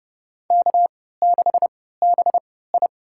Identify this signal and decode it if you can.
Morse code, heard as a keyed tone, about 32 wpm, spaced as for 17 wpm, 700 Hz. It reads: K6BI
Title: USA Call Signs - 4 Characters - 3x Character Spacing - 30wpm